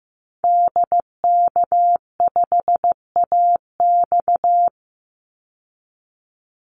Morse code, heard as a keyed tone, about 15 wpm, 700 Hz. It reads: DK5AX
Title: Call Signs Repeated 2x - 15wpm